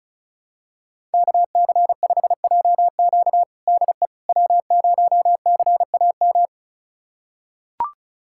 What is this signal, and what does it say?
Morse code, heard as a keyed tone, about 36 wpm, 700 Hz. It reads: KC5JQ DE W0CAM
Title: USA Callsign DE Callsign - 5 Characters - 35wpm